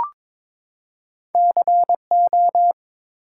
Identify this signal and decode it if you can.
Morse code, heard as a keyed tone, about 22 wpm, 700 Hz. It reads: CO
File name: practice / 22wpm / Two Letter-Number Combinations - 22wpm